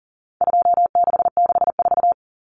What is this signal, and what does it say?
Morse code, heard as a keyed tone, about 40 wpm, 700 Hz. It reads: 2664
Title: Common Cut Numbers - 4-Digit 40wpm